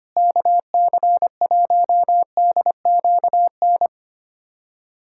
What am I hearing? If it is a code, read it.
Morse code, heard as a keyed tone, about 25 wpm, 700 Hz. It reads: KC1DQN
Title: USA Callsign DE Callsign 25wpm